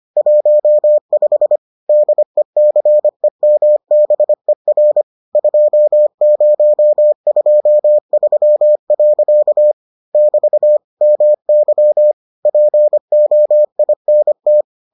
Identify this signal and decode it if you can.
Morse code, heard as a keyed tone, about 25 wpm, 600 Hz. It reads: 15 DECEMBER 2023. <BT> MY POINT